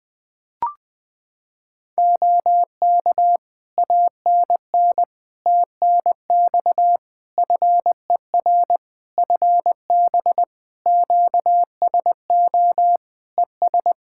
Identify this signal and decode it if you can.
Morse code, heard as a keyed tone, about 21 wpm, 700 Hz. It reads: OK ANN TNX FER FB QSO ES
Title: K1USN SST QSO Practice 20wpm